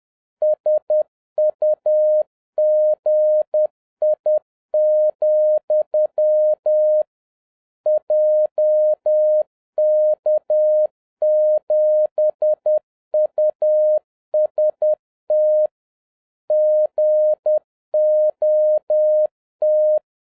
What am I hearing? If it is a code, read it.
Morse code, heard as a keyed tone, about 10 wpm, 610 Hz. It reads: SUGI, JK7UST GOT
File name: 2011-07-mp3 / n36 THE 3.11 2 10wpm